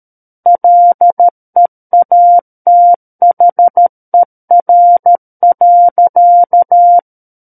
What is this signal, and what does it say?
Morse code, heard as a keyed tone, about 13 wpm, 700 Hz. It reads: LEATHER.
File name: Book One / 04-Chapter 4 - 13 WPM